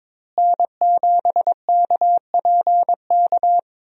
Morse code, heard as a keyed tone, about 22 wpm, 700 Hz. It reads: N7KPK
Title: USA Call Signs - 5 Characters - Repeated 2x - 22wpm